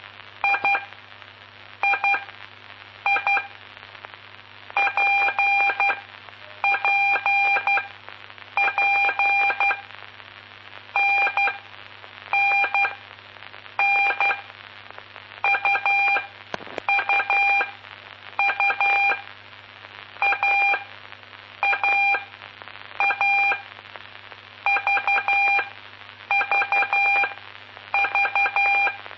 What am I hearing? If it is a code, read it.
Morse code, heard as a keyed tone, about 11 wpm, spaced as for 7 wpm, 2400 Hz, with noise. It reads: III PPP NNN UUU AAA VVV
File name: aulas de telegrafia